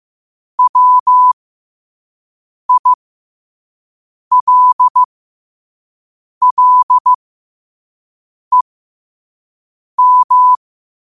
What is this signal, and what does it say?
Morse code, heard as a keyed tone, about 15 wpm, spaced as for 5 wpm, 1000 Hz. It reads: WILLEM
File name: qotdfiles / 5wpm